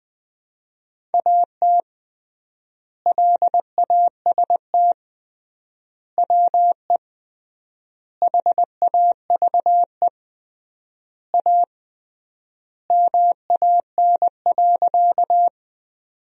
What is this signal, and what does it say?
Morse code, heard as a keyed tone, about 19 wpm, 700 Hz. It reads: AT LAST WE HAVE A MAN.
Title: 3x Word-Spacing - Sentences from Top 300 Words - 20wpm